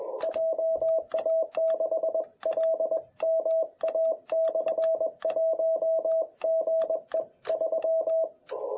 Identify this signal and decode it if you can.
Morse code, heard as a keyed tone, about 21 wpm, 650 Hz, with radio noise. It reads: JA6LMA/1 GE 3
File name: jh8cbh / 2021cw